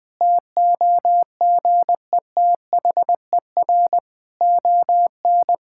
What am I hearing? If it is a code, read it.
Morse code, heard as a keyed tone, about 19 wpm, 700 Hz. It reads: TOGETHER ON
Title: Difficult Sentences with Hints - 20wpm